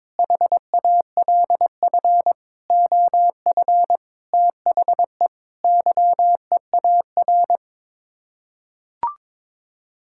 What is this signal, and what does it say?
Morse code, heard as a keyed tone, about 22 wpm, 700 Hz. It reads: HALF OF THE YEAR
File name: practice / 22wpm / Sets of 7 Words - 22wpm